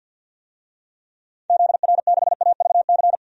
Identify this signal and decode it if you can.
Morse code, heard as a keyed tone, about 53 wpm, 700 Hz. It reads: ZR6AVC
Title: DXCC-100 Call Signs - 6 Characters - 50wpm